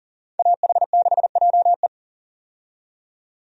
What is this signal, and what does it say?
Morse code, heard as a keyed tone, about 40 wpm, 700 Hz. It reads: AH6JE
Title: USA Call Signs - 5 Characters - 40wpm